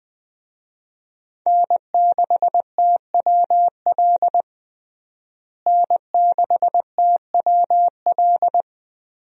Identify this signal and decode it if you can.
Morse code, heard as a keyed tone, about 20 wpm, 700 Hz. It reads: N6TWL N6TWL
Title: USA Call Signs - 5 Characters - Repeated 3x - 20wpm